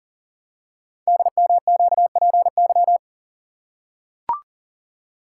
Morse code, heard as a keyed tone, about 43 wpm, 700 Hz. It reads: DMQPY
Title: Code Groups - 5 Letters - 40wpm